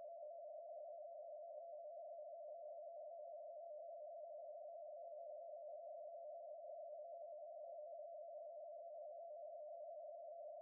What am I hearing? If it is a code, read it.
Two-tone teleprinter signal, about 80 Hz shift.